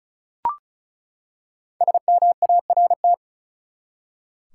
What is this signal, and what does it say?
Morse code, heard as a keyed tone, about 36 wpm, 700 Hz. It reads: SMART
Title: Five Letter Words - 35wpm